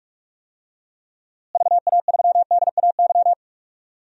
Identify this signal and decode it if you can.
Morse code, heard as a keyed tone, about 41 wpm, 700 Hz. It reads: VA3DAY
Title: DXCC-100 Call Signs - 6 Characters - Repeated 3x - 45wpm